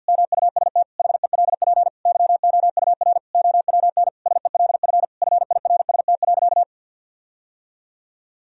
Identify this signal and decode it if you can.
Morse code, heard as a keyed tone, about 49 wpm, 700 Hz. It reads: MUST HELP YOUR OWN SELF FIRST.
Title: Sentences from Top 1000 Words - Review Part 1 - 50wpm